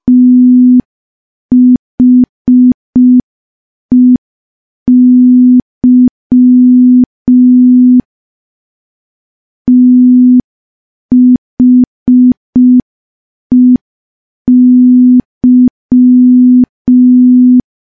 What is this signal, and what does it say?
Morse code, heard as a keyed tone, about 5 wpm, 250 Hz. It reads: THEY THEY
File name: morse / morse-training-05WPM